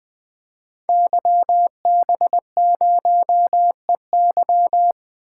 Morse code, heard as a keyed tone, about 20 wpm, 700 Hz. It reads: YB0EY
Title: Code Groups - 5 Letter:Numbers - 20wpm